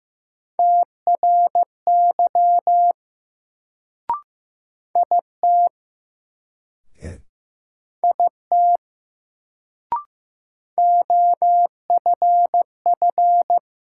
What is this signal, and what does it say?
Morse code, heard as a keyed tone, about 15 wpm, 700 Hz. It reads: TRY IT IT OFF
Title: Sets of 2 Words - New Word Drill - 15wpm